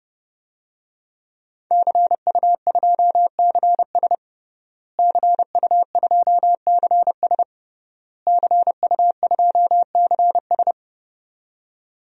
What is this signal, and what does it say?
Morse code, heard as a keyed tone, about 30 wpm, 700 Hz. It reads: CU2CS CU2CS CU2CS
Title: DXCC-100 Call Signs - 5 Characters - Repeated 3x - 30wpm